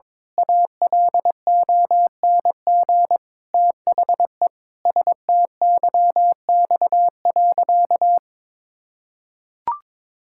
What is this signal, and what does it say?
Morse code, heard as a keyed tone, about 22 wpm, 700 Hz. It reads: ALONG THE STYX.
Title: Pangram Challenge v2 - 22wpm